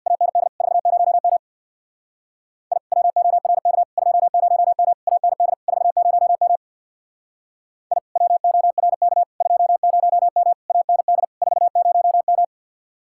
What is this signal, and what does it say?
Morse code, heard as a keyed tone, about 62 wpm, 700 Hz. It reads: AND 40M I WORK 20M AND 40M I WORK 20M AND 40M
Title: Speed-Racing Common QSO Phrases - 50wpm